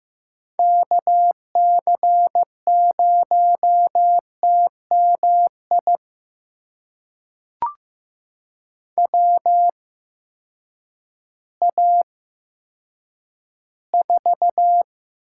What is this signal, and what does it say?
Morse code, heard as a keyed tone, about 15 wpm, spaced as for 9 wpm, 700 Hz. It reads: KC0TMI WA4
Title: USA Call Signs - 6 Characters - 8x Character Spacing - 15wpm